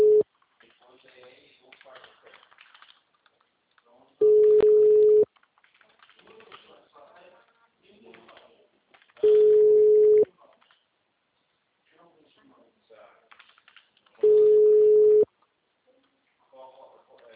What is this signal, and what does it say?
Ringback tone (Europe, 425 Hz), with speech.